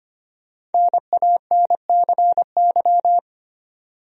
Morse code, heard as a keyed tone, about 25 wpm, 700 Hz. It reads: NANCY
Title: USA Common Names - 25wpm